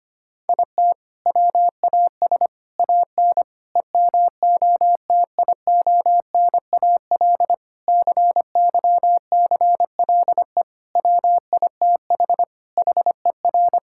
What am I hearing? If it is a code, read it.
Morse code, heard as a keyed tone, about 25 wpm, 700 Hz. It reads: IT WAS AN EMOTIONAL CYCLE WITH HER
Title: Sentences from Top 1000 Words - Review Part 6 - 25wpm